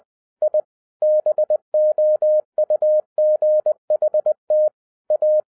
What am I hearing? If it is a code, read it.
Morse code, heard as a keyed tone, about 20 wpm, 610 Hz. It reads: I BOUGHT A
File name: 2012-01-mp3 / n05-CAME BACK TO HAM RADIO AFTER 30 YEARS OF INTERVAL-20wpm